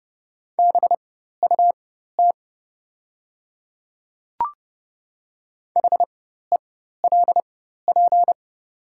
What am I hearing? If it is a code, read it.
Morse code, heard as a keyed tone, about 30 wpm, spaced as for 14 wpm, 700 Hz. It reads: BUT HELP
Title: 4x Character Spacing - Top 300 Words - 30wpm